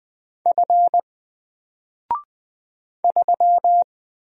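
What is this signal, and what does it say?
Morse code, heard as a keyed tone, about 20 wpm, 700 Hz. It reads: F 3
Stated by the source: Single letter and number Morse drill at 20 wpm, from N4LCD.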